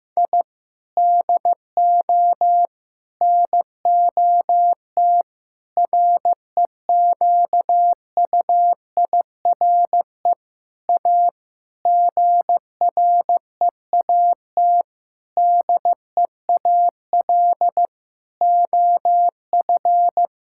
Morse code, heard as a keyed tone, about 15 wpm, 700 Hz. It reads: I DO NOT REQUIRE A GREAT DEAL OF
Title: Sentences from Top 1000 Words - Review Part 6 - 15wpm